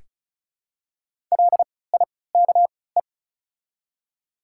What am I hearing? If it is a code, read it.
Morse code, heard as a keyed tone, about 33 wpm, spaced as for 19 wpm, 700 Hz. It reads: LIKE
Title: 3x Character Spacing - Top 100 Words - 35wpm